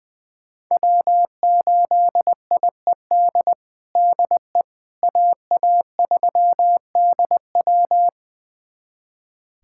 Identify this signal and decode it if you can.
Morse code, heard as a keyed tone, about 20 wpm, 700 Hz. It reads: W8IED DE AA3DW
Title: USA Callsign DE Callsign - 5 Characters - 20wpm